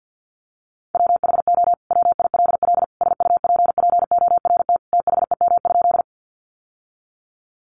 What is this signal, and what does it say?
Morse code, heard as a keyed tone, about 50 wpm, 700 Hz. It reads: WHO WILL SUPPORT THEM?
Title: Sentences from Top 800 Words 50wpm